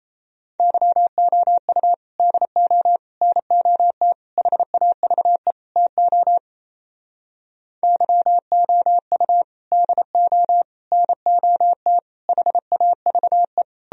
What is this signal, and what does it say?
Morse code, heard as a keyed tone, about 37 wpm, 700 Hz. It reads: YOU DO NOT HAVE TO YOU DO NOT HAVE